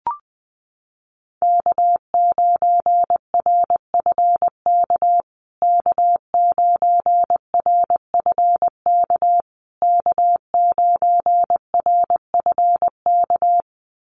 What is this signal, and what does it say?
Morse code, heard as a keyed tone, about 19 wpm, 700 Hz. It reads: K9RFK K9RFK K9RFK
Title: USA Call Signs Repeated 3x 20wpm